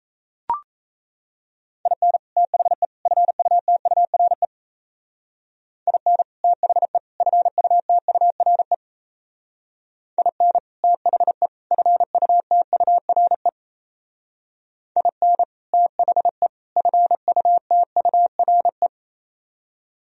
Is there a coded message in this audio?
Yes: Morse code from 1.9 to 18.9 s, heard as a keyed tone, about 35 wpm, 700 Hz. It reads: IN THE FUTURE IN THE FUTURE IN THE FUTURE IN THE FUTURE